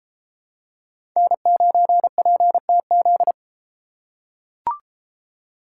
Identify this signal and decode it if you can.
Morse code, heard as a keyed tone, about 31 wpm, 700 Hz. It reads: N9PTZ